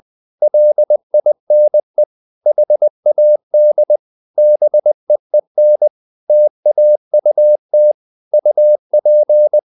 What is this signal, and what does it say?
Morse code, heard as a keyed tone, about 20 wpm, 600 Hz. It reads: LINE HAD BEEN TAUT UP
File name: the old man and the sea / the old man and the sea part 1 20wpm